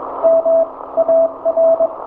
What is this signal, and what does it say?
Morse code, heard as a keyed tone, about 22 wpm, spaced as for 18 wpm, 660 Hz. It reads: MAR
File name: hamradio / last 500khz cw